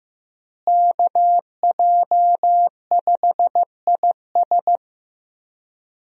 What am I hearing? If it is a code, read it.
Morse code, heard as a keyed tone, about 15 wpm, 700 Hz. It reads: KJ5IS
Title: USA Call Signs - 5 Characters - Repeated 2x - 15wpm